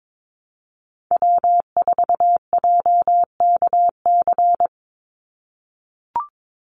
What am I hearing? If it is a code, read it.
Morse code, heard as a keyed tone, about 22 wpm, 700 Hz. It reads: W4JKC
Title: NA SOTA Chaser Callsigns 22wpm